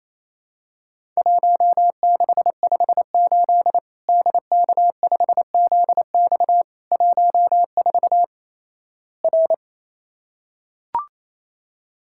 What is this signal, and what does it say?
Morse code, heard as a keyed tone, about 28 wpm, 700 Hz. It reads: 1658 DK5ZX 14 R
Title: Worked All Europe DX Contest - QTC Exchange - 28wpm